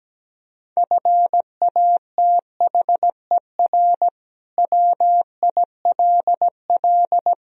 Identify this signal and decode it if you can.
Morse code, heard as a keyed tone, about 17 wpm, 700 Hz. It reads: FATHER WILL
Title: Sentences from Top 1000 Words - Review Part 3 - 17wpm